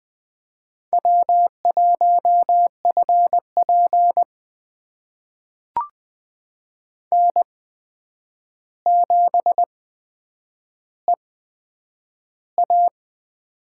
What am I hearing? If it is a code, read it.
Morse code, heard as a keyed tone, about 20 wpm, spaced as for 9 wpm, 700 Hz. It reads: W1FP N7EA
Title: USA Call Signs - 4 Characters - 8x Character Spacing - 20wpm